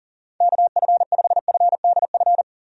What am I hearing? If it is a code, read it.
Morse code, heard as a keyed tone, about 37 wpm, 700 Hz. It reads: KF5FDF
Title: USA Call Signs - 6 Characters - 40wpm